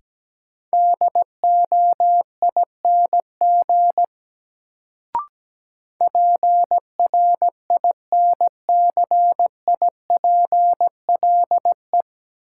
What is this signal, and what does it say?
Morse code, heard as a keyed tone, about 17 wpm, 700 Hz. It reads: DOING PRINCIPLE
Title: Sentences from Top 1400 Words - New Word Drill Alt - 17wpm